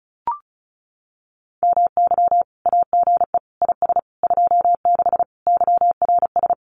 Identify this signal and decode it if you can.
Morse code, heard as a keyed tone, about 36 wpm, 700 Hz. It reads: MY AGE IS 26 YRS
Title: Common QSO Phrases Encore 35wpm